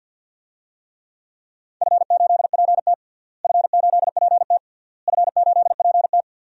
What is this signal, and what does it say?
Morse code, heard as a keyed tone, about 51 wpm, 700 Hz. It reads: F8PT F8PT F8PT
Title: DXCC-100 Call Signs - 4 Characters - Repeated 3x - 50wpm